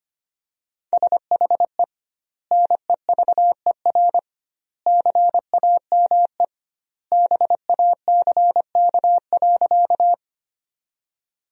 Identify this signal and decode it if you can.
Morse code, heard as a keyed tone, about 25 wpm, 700 Hz. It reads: SHE NEVER CAME BACK.
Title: 2x Word-Spacing - Sentences from Top 300 Words - 25wpm